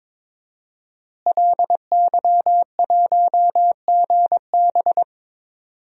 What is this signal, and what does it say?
Morse code, heard as a keyed tone, about 22 wpm, 700 Hz. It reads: LY1GB
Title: DXCC-100 Call Signs - 5 Characters - Repeated 3x - 22wpm